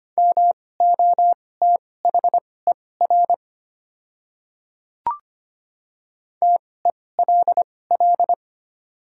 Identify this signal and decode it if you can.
Morse code, heard as a keyed tone, about 25 wpm, spaced as for 18 wpm, 690 Hz. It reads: MOTHER TELL
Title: 2x Character Spacing - Top 300 Words - 25wpm